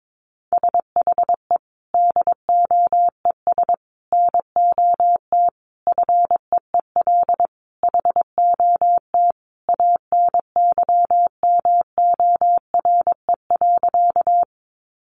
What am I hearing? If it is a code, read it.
Morse code, heard as a keyed tone, about 22 wpm, 700 Hz. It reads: SHE DOES NOT FEEL HOT ANYMORE.
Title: Easy Phrases 22wpm